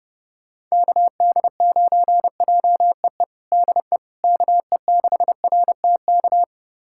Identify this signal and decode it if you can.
Morse code, heard as a keyed tone, about 30 wpm, 700 Hz. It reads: KD9JEE DE KE6RTK
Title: USA Callsign DE Callsign - 6 Characters - 30wpm